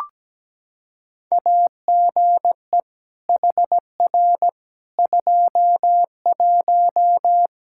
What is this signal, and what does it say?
Morse code, heard as a keyed tone, about 17 wpm, 700 Hz. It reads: AGE HR 21
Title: K1USN SST QSO Practice 17wpm